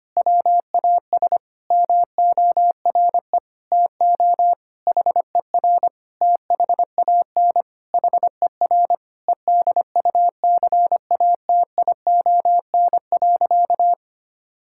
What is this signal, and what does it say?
Morse code, heard as a keyed tone, about 25 wpm, 700 Hz. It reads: WAS MORE TO HER THAN HER EDUCATION.
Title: Sentences from Top 1000 Words - Review Part 6 - 25wpm